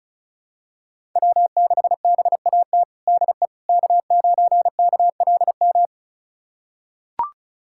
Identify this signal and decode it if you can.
Morse code, heard as a keyed tone, about 36 wpm, 700 Hz. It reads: W6BAT DE K9KLM
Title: USA Callsign DE Callsign - 5 Characters - 35wpm